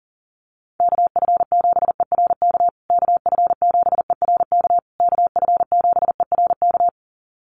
Morse code, heard as a keyed tone, about 40 wpm, 700 Hz. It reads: KF7ERK KF7ERK KF7ERK
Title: USA Call Signs Repeated 3x 40wpm